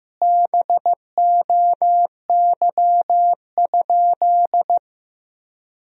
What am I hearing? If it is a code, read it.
Morse code, heard as a keyed tone, about 15 wpm, 700 Hz. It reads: BOY?
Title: Sentences from Top 1000 Words - Review Part 4 - 15wpm